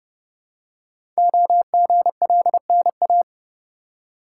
Morse code, heard as a keyed tone, about 29 wpm, 700 Hz. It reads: OGLNA
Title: Code Groups - 5 Letters - 30wpm